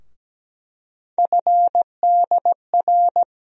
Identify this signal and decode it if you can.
Morse code, heard as a keyed tone, about 17 wpm, 700 Hz. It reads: FDR
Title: USA Call Sign Postfixes - 3 Letters 17wpm